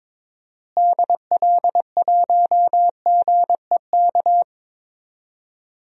Morse code, heard as a keyed tone, about 22 wpm, 700 Hz. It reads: DL1GEK
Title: DXCC-100 Call Signs - 6 Characters - Repeated 2x - 22wpm